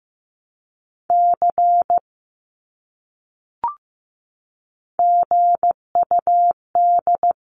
Morse code, heard as a keyed tone, about 15 wpm, 700 Hz. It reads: C GUD
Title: Abbreviations Repeated 3x - 15wpm